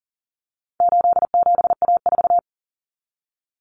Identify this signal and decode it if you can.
Morse code, heard as a keyed tone, about 40 wpm, 700 Hz. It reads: 87A4
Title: Common Cut Numbers - 4-Digit 40wpm